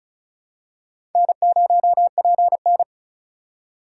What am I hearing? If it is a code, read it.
Morse code, heard as a keyed tone, about 36 wpm, 700 Hz. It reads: N0PN
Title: USA Call Signs - 2x Character Spacing - 35wpm